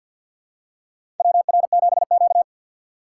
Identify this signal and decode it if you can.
Morse code, heard as a keyed tone, about 52 wpm, 700 Hz. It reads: WR7Q
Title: USA Call Signs - 4x Character Spacing - 50wpm